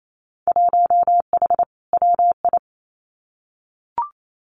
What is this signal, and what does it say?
Morse code, heard as a keyed tone, about 28 wpm, 700 Hz. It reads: 1H WI
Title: Winter Field Day Exchanges 28wpm